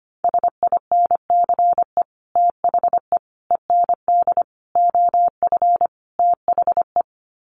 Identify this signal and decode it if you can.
Morse code, heard as a keyed tone, about 25 wpm, 700 Hz. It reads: SINCE THE END OF THE
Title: Sets of 5 Words Encore 25wpm